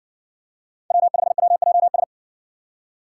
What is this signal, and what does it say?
Morse code, heard as a keyed tone, about 57 wpm, 700 Hz. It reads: W5WJS